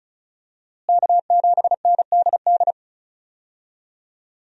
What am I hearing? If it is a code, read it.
Morse code, heard as a keyed tone, about 36 wpm, 700 Hz. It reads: K7NDD
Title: USA Call Signs - 5 Characters - Repeated 3x - 35wpm